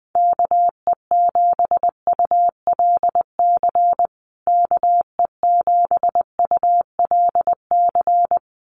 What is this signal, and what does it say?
Morse code, heard as a keyed tone, about 20 wpm, 700 Hz. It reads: KE7ULC KE7ULC
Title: USA Call Signs Repeated 2x 20wpm